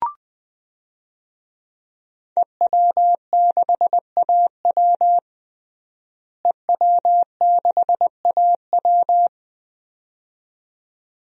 Morse code, heard as a keyed tone, about 21 wpm, 700 Hz. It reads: EW6AW EW6AW
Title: DXCC-100 Call Signs - 5 Characters - Repeated 2x - 20wpm